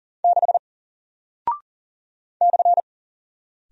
Morse code, heard as a keyed tone, about 40 wpm, 700 Hz. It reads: 6 /
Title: Single Character (Letter:Number:Punctuation:Prosign) 40wpm